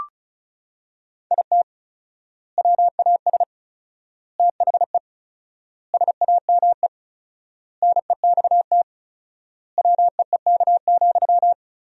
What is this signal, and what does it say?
Morse code, heard as a keyed tone, about 36 wpm, 700 Hz. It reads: IT WAS THE SAME NEXT WEEK,
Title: 4x Word-Spacing - Sentences from Top 500 Words - 35wpm